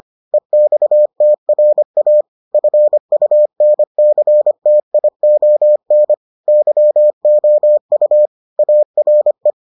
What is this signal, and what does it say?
Morse code, heard as a keyed tone, about 25 wpm, 600 Hz. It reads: EXTRA FUNCTION YOU ARE